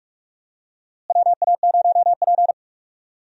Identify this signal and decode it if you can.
Morse code, heard as a keyed tone, about 46 wpm, 700 Hz. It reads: WA0P
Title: USA Call Signs - 4 Characters - Repeated 2x - 45wpm